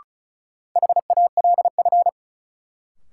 Morse code, heard as a keyed tone, about 36 wpm, 700 Hz. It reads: HALF
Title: Sets of 3 Words - New Word Drill - 35wpm